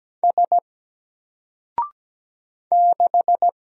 Morse code, heard as a keyed tone, about 17 wpm, 700 Hz. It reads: S 6
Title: Single Character (Letter:Number:Punctuation:Prosign) Alt 17wpm